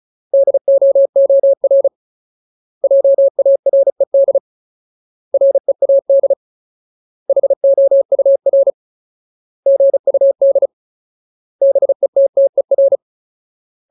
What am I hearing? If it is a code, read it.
Morse code, heard as a keyed tone, about 34 wpm, 550 Hz. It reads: DOOR JARED READ HOUR GUD BETTER
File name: Ultimate Word List / 35WPM Ultimate